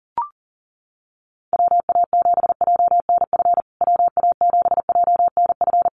Morse code, heard as a keyed tone, about 40 wpm, 700 Hz. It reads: WA7JNF WA7JNF
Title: USA Call Signs Repeated 3x 40wpm